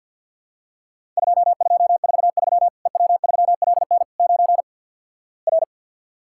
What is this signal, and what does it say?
Morse code, heard as a keyed tone, about 50 wpm, 700 Hz. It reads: 2243 EW3LN 9 R
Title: Worked All Europe DX Contest - QTC Exchange - 50wpm